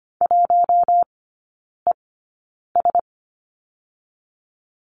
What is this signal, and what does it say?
Morse code, heard as a keyed tone, about 25 wpm, spaced as for 9 wpm, 700 Hz. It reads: 1ES
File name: course / 25:10wpm / 093 Lesson 16e - Letter c - Three Characters 25wpm